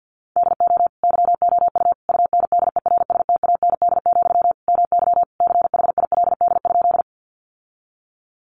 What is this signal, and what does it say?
Morse code, heard as a keyed tone, about 50 wpm, 700 Hz. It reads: DO YOU UNDERSTAND, MY CHILD?
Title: Sentences from Top 800 Words 50wpm